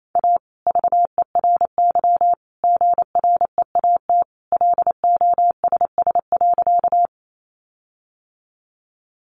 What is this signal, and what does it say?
Morse code, heard as a keyed tone, about 28 wpm, 700 Hz. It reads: A VERY GREAT LOSS.
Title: Sentences from Top 1400 Words 28wpm